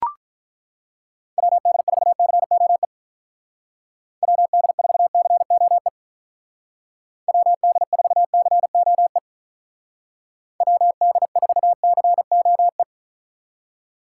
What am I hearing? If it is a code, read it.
Morse code, heard as a keyed tone, about 42 wpm, 700 Hz. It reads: WD4COE WD4COE WD4COE WD4COE
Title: Speed Racing - USA Call Signs 35wpm